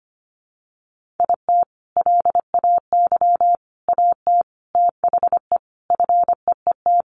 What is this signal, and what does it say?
Morse code, heard as a keyed tone, about 25 wpm, 700 Hz. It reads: IT LAY AT THE FEET
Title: Sentences from Top 1500 Words 25wpm